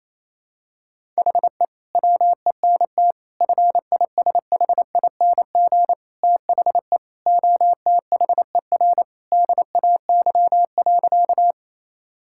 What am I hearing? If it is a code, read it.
Morse code, heard as a keyed tone, about 27 wpm, 700 Hz. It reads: HE WENT FISHING THE OTHER DAY.
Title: Sentences from Top 1000 Words - Review Part 5 - 28wpm